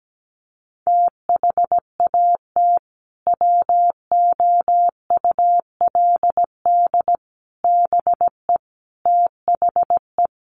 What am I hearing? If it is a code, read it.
Morse code, heard as a keyed tone, about 17 wpm, 700 Hz. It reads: THAT WOULD BE THE
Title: Sentences from Top 400 Words 17wpm